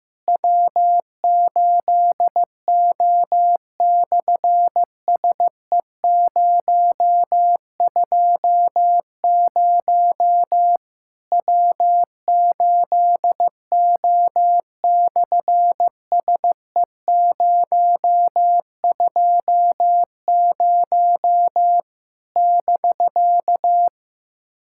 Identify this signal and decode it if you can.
Morse code, heard as a keyed tone, about 15 wpm, 700 Hz. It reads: W8O/SE020 W8O/SE020 <BK>
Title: SOTA QSO Practice 15wpm